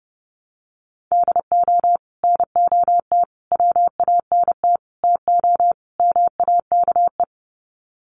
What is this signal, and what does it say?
Morse code, heard as a keyed tone, about 30 wpm, 700 Hz. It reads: DO NOT WANT TO MAKE
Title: Sets of 5 Words Encore 30wpm